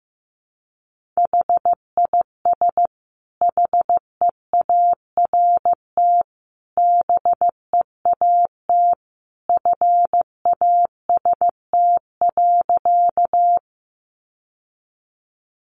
Morse code, heard as a keyed tone, about 15 wpm, 700 Hz. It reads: HIS HEART BEAT FAST.
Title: Sentences from Top 1300 Words 15wpm